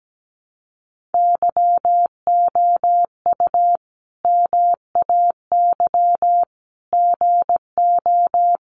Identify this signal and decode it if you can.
Morse code, heard as a keyed tone, about 17 wpm, 700 Hz. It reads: YOU MAY GO
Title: Sentences from Top 1500 Words 17wpm